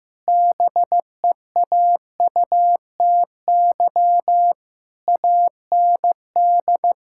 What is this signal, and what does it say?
Morse code, heard as a keyed tone, about 16 wpm, 700 Hz. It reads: BEAUTY AND
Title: Sentences from Top 1600 Words 15wpm